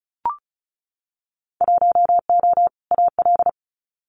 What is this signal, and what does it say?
Morse code, heard as a keyed tone, about 34 wpm, 700 Hz. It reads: 1O AL
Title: Winter Field Day Exchanges 35wpm